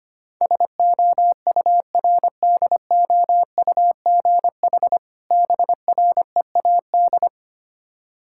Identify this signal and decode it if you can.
Morse code, heard as a keyed tone, about 25 wpm, 700 Hz. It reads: SOURDOUGH BREAD
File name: practice / 25wpm / Difficult Words with Hints - 25wpm